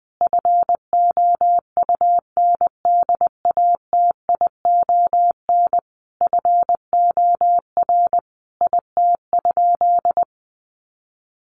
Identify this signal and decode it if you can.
Morse code, heard as a keyed tone, about 20 wpm, 700 Hz. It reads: FOUNDATION FOR IT?
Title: Sentences from Top 1200 Words 20wpm